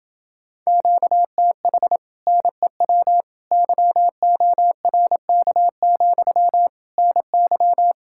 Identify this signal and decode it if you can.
Morse code, heard as a keyed tone, about 28 wpm, 700 Hz. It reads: QTH NEW YORK, NY